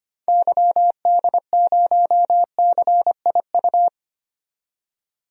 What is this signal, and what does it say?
Morse code, heard as a keyed tone, about 25 wpm, 700 Hz. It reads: YD0CIU
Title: DXCC-100 Call Signs - 6 Characters - 25wpm